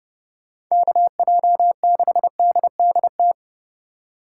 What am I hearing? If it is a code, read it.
Morse code, heard as a keyed tone, about 33 wpm, 700 Hz. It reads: KJ6DDT